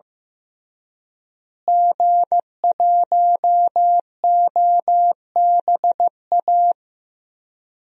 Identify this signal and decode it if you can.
Morse code, heard as a keyed tone, about 15 wpm, 700 Hz. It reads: G1OBA
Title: DXCC-100 Call Signs - 5 Characters - Repeated 3x - 15wpm